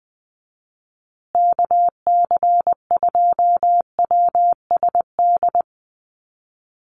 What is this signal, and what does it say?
Morse code, heard as a keyed tone, about 20 wpm, 700 Hz. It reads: KC2WSD
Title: USA Call Signs Repeated 3x 20wpm